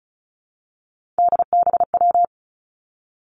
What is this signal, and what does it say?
Morse code, heard as a keyed tone, about 36 wpm, 700 Hz. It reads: DBW
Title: Three Challenging Letter-Number Combinations 35wpm